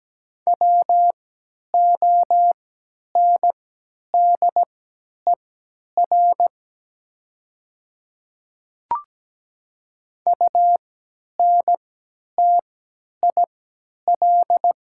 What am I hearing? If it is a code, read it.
Morse code, heard as a keyed tone, about 17 wpm, spaced as for 10 wpm, 700 Hz. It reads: WONDER UNTIL
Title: 3x Character Spacing - Top 500 Words - 17wpm